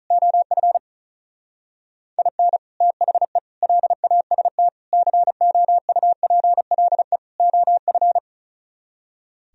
Morse code, heard as a keyed tone, about 45 wpm, 700 Hz. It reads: OF IN THE LAST COUPLE OF